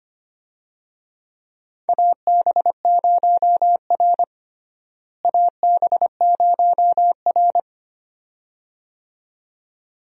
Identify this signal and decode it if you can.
Morse code, heard as a keyed tone, about 25 wpm, 700 Hz. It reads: AB0R AB0R
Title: USA Call Signs - 4 Characters - Repeated 2x - 25wpm